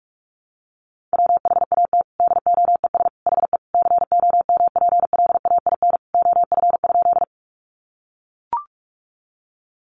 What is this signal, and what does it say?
Morse code, heard as a keyed tone, about 44 wpm, 700 Hz. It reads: WHAT DOES HE COMPLAIN OF?
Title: Sentences from Top 1300 Words 45wpm